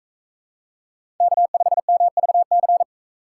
Morse code, heard as a keyed tone, about 42 wpm, 700 Hz. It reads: K5MVC